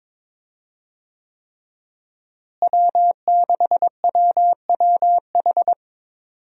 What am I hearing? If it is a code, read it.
Morse code, heard as a keyed tone, about 22 wpm, 700 Hz. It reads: W6WWH
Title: USA Call Signs - 5 Characters - Repeated 3x - 22wpm